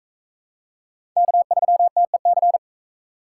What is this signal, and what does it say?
Morse code, heard as a keyed tone, about 39 wpm, 700 Hz. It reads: K3TEC